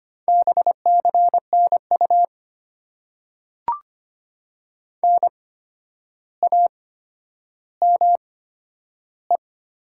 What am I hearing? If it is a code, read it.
Morse code, heard as a keyed tone, about 25 wpm, spaced as for 11 wpm, 700 Hz. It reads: BCNU NAME
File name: Top 90 QSO Elements - 8x Character Spacing - 25wpm